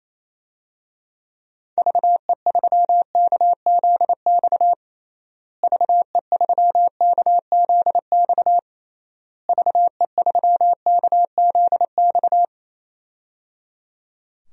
Morse code, heard as a keyed tone, about 28 wpm, 700 Hz. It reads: VE3KZX VE3KZX VE3KZX
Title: DXCC-100 Call Signs - 6 Characters - Repeated 3x - 28wpm